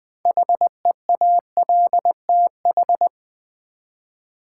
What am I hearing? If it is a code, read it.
Morse code, heard as a keyed tone, about 20 wpm, 700 Hz. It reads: HEALTH
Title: Six Letter Words - 20wpm